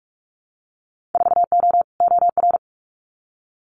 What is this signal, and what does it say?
Morse code, heard as a keyed tone, about 46 wpm, 700 Hz. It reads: 4O OR
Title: Winter Field Day Exchanges Encore 45wpm